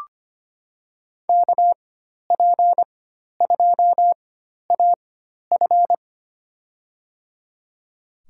Morse code, heard as a keyed tone, about 25 wpm, spaced as for 12 wpm, 700 Hz. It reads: KP2AF
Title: USA Call Signs - 5 Characters - 4x Character Spacing - 25wpm